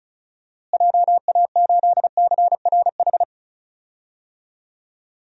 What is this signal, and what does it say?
Morse code, heard as a keyed tone, about 34 wpm, 700 Hz. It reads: JA8CRH
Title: DXCC-100 Call Signs - 6 Characters - Repeated 3x - 35wpm